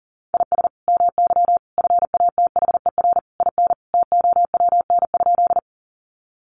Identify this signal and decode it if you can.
Morse code, heard as a keyed tone, about 40 wpm, 700 Hz. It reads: IS MY FATHER IN TOWN?
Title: Sentences from Top 500 Words 40wpm